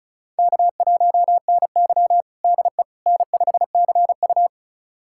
Morse code, heard as a keyed tone, about 34 wpm, 700 Hz. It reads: K1NY DE N5CU
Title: USA Callsign DE Callsign - 4 Characters - 35wpm